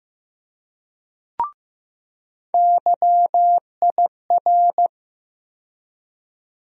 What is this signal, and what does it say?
Morse code, heard as a keyed tone, about 15 wpm, 700 Hz. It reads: YIR
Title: USA Call Sign Postfixes - 3 Letters 15wpm